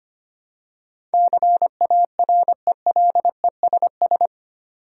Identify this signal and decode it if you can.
Morse code, heard as a keyed tone, about 25 wpm, 700 Hz. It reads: CARELESS
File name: Eight Letter Words - 25wpm